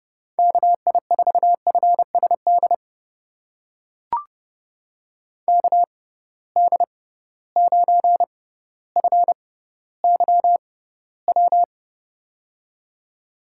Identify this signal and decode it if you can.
Morse code, heard as a keyed tone, about 30 wpm, spaced as for 15 wpm, 700 Hz. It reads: KI4FSD KD9FYW
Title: USA Call Signs - 6 Characters - 6x Character Spacing - 30wpm